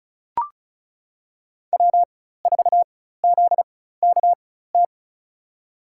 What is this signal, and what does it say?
Morse code, heard as a keyed tone, about 36 wpm, spaced as for 16 wpm, 700 Hz. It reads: W4ZKT
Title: USA Call Signs - 5 Characters - 4x Character Spacing - 35wpm